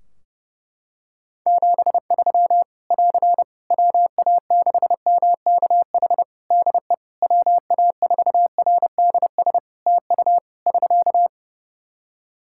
Morse code, heard as a keyed tone, about 30 wpm, 700 Hz. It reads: 73 <AR> WA6MKH DE WA4RDS TU <SK>
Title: Practice QSO - Exchange 4 - Ending - 30wpm